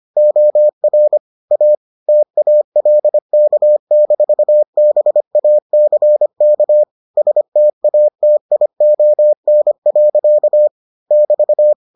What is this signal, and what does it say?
Morse code, heard as a keyed tone, about 25 wpm, 600 Hz. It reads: OR A TALK-BACK STATION. <BT>